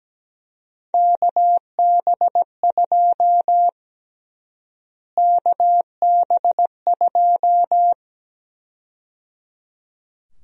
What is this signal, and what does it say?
Morse code, heard as a keyed tone, about 17 wpm, 700 Hz. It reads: KB2 KB2
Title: USA Call Sign Prefixes - Repeated 2x - 17wpm